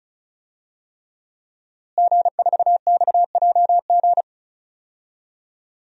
Morse code, heard as a keyed tone, about 34 wpm, 700 Hz. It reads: G4XJG
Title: DXCC-100 Call Signs - 5 Characters - Repeated 3x - 35wpm